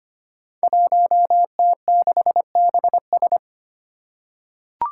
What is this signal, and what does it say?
Morse code, heard as a keyed tone, about 25 wpm, 700 Hz. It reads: 1T6BS
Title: Five Challenging Letter-Number Combinations 25wpm